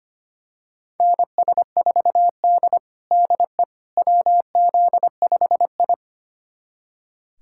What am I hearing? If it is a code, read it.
Morse code, heard as a keyed tone, about 25 wpm, 700 Hz. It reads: NS4D DE WZ5I
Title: USA Callsign DE Callsign - 4 Characters - 25wpm